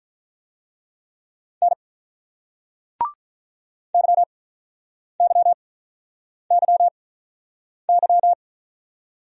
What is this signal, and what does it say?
Morse code, heard as a keyed tone, about 44 wpm, spaced as for 8 wpm, 700 Hz. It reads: N YYYY